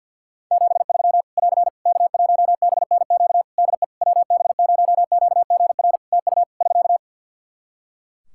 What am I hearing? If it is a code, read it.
Morse code, heard as a keyed tone, about 52 wpm, 700 Hz. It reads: 73 <AR> K1BNQ DE WB0QGR TU <SK>
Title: Practice QSO - Exchange 4 - Ending - 50wpm